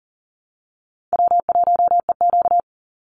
Morse code, heard as a keyed tone, about 40 wpm, 700 Hz. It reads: W1EQ
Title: Easy Call Signs Repeated 3x - 40wpm